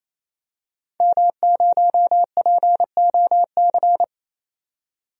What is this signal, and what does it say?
Morse code, heard as a keyed tone, about 28 wpm, 700 Hz. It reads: M0POC
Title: DXCC-100 Call Signs - 5 Characters - 28wpm